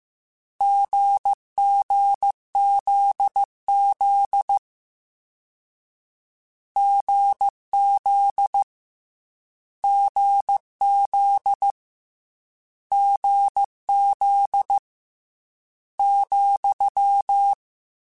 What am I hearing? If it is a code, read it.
Morse code, heard as a keyed tone, about 15 wpm, 800 Hz. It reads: GGZZ GZ GZ GZ ,
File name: Difficult Combos / Difficult Combos-mp3s / GZ GZ7 @13